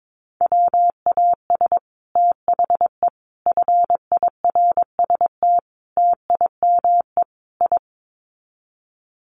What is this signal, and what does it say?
Morse code, heard as a keyed tone, about 22 wpm, 700 Hz. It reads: WAS THE FIRST TIME I
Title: Sets of 5 Words Encore 22wpm